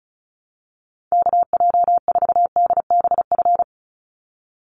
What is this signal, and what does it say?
Morse code, heard as a keyed tone, about 34 wpm, 700 Hz. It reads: KJ4DBF
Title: USA Call Signs 35wpm